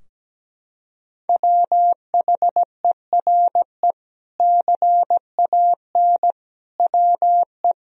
Morse code, heard as a keyed tone, about 17 wpm, 700 Hz. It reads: WHERE CAN WE
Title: Sentences from Top 1000 Words - Review Part 2 - 17wpm